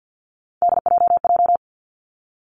Morse code, heard as a keyed tone, about 50 wpm, 700 Hz. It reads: DJJ
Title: Three Challenging Letter-Number Combinations 50wpm